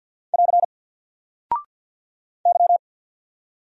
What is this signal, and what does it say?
Morse code, heard as a keyed tone, about 49 wpm, 700 Hz. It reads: <AR> Y
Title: Single Character (Letter:Number:Punctuation:Prosign) 50wpm